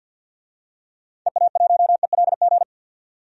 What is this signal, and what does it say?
Morse code, heard as a keyed tone, about 45 wpm, 700 Hz. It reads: EA1ELG
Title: DXCC-100 Call Signs - 6 Characters - 50wpm